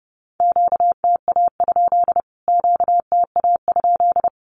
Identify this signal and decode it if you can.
Morse code, heard as a keyed tone, about 30 wpm, 700 Hz. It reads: QTA? QTA?
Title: Q-Codes Repeated 3x 30wpm